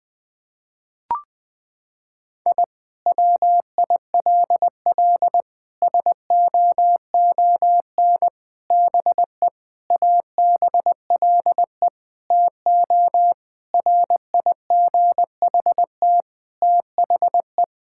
Morse code, heard as a keyed tone, about 21 wpm, 700 Hz. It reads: I WILL SOON BE ABLE TO RIGHT THE
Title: Sentences from Top 1000 Words - Review Part 6 - 20wpm